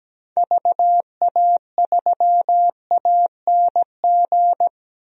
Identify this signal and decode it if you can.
Morse code, heard as a keyed tone, about 17 wpm, 700 Hz. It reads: VA3ANG
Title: DXCC-100 Call Signs - 6 Characters - 17wpm